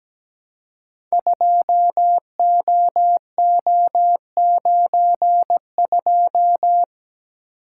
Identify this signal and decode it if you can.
Morse code, heard as a keyed tone, about 17 wpm, 700 Hz. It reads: 2OO92
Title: Code Groups - 5 Letter:Numbers - 17wpm